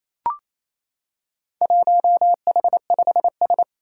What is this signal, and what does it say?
Morse code, heard as a keyed tone, about 27 wpm, 700 Hz. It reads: 1H5S
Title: Four Challenging Letter-Number Combinations 28wpm